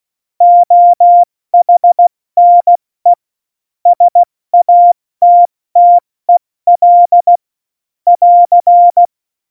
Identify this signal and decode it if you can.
Morse code, heard as a keyed tone, about 16 wpm, 700 Hz. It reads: OHNE SATTEL <AR>
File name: Morsekurs / Witze / Witz 11 August 2020, 70 F80